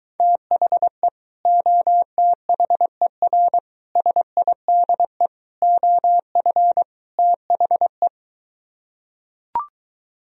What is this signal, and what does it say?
Morse code, heard as a keyed tone, about 24 wpm, 700 Hz. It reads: THE OTHER SIDE OF THE